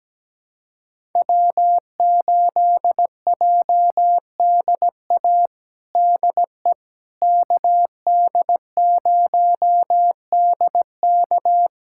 Morse code, heard as a keyed tone, about 17 wpm, 700 Hz. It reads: W8JDA DE KD0DK
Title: USA Callsign DE Callsign - 5 Characters - 17wpm